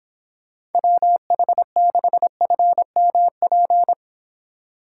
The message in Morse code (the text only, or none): WH6FMP